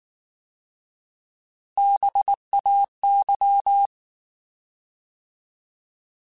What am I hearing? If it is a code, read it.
Morse code, heard as a keyed tone, about 19 wpm, 800 Hz. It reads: BAY
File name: Alphabetical / Alphabetical mp3s / Bs @18